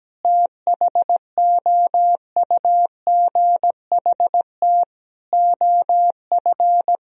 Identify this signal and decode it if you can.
Morse code, heard as a keyed tone, about 17 wpm, 700 Hz. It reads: THOUGHT OF
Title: Sentences from Top 1000 Words - Review Part 1 - 17wpm